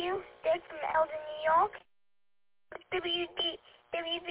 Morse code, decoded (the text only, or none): none